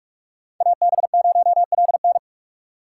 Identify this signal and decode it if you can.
Morse code, heard as a keyed tone, about 44 wpm, 700 Hz. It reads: AB0LN
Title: USA Call Signs - 5 Characters - 3x Character Spacing - 45wpm